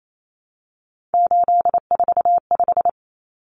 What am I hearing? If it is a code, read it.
Morse code, heard as a keyed tone, about 28 wpm, 700 Hz. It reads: 845
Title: Common Cut Numbers - 3-Digit 28wpm